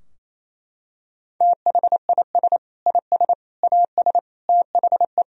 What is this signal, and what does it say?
Morse code, heard as a keyed tone, about 27 wpm, 700 Hz. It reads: THIS IS AS THE
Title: Sentences from Top 1000 Words - Review Part 3 - 28wpm